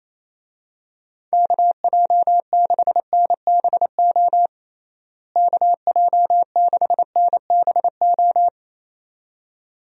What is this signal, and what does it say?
Morse code, heard as a keyed tone, about 28 wpm, 700 Hz. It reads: KJ6NBO KJ6NBO
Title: USA Call Signs - 6 Characters - Repeated 2x - 28wpm